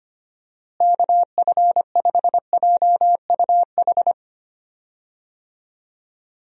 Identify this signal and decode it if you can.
Morse code, heard as a keyed tone, about 25 wpm, 700 Hz. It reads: KF5JUH
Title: USA Call Signs - 6 Characters - Repeated 2x - 25wpm